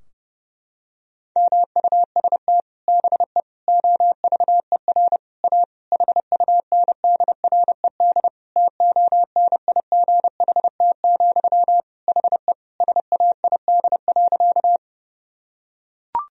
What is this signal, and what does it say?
Morse code, heard as a keyed tone, about 32 wpm, 700 Hz. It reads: MUST BE OVER A HUNDRED TONIGHT, HE SAID.
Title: Sentences from Top 1000 Words - Review Part 2 - 30wpm